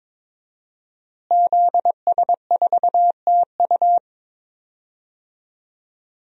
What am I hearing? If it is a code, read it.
Morse code, heard as a keyed tone, about 21 wpm, 700 Hz. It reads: ZS4TU
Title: DXCC-100 Call Signs - 5 Characters - Repeated 3x - 22wpm